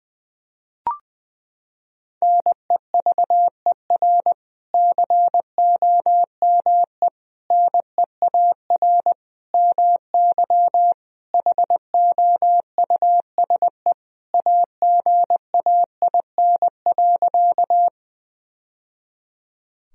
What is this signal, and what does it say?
Morse code, heard as a keyed tone, about 20 wpm, 700 Hz. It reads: NEVER COME NEAR MY HOUSE AGAIN.
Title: Sentences from Top 1000 Words - Review Part 1 - 20wpm